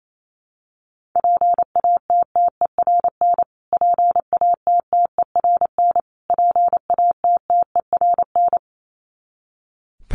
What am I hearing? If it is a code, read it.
Morse code, heard as a keyed tone, about 28 wpm, 700 Hz. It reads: PATTERN PATTERN PATTERN
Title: Top 500 Words Repeated 28wpm